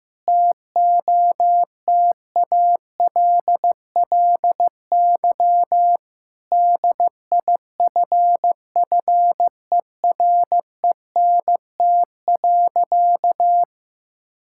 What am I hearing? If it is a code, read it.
Morse code, heard as a keyed tone, about 15 wpm, 700 Hz. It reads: TOTALLY DIFFERENT.
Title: Sentences from Top 1600 Words 15wpm